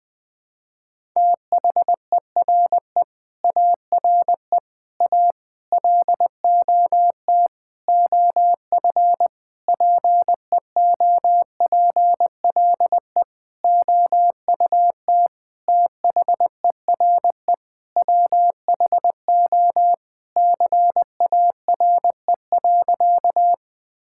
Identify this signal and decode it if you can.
Morse code, heard as a keyed tone, about 20 wpm, 700 Hz. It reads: THERE ARE A LOT OF PEOPLE OUT THERE WHO CARE.
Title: Sentences with 9-Word Phrases - Spoken Beginning - 20wpm